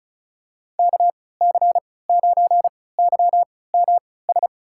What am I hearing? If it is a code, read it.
Morse code, heard as a keyed tone, about 36 wpm, spaced as for 20 wpm, 700 Hz. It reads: KC9YMS
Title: USA Call Signs - 6 Characters - 3x Character Spacing - 35wpm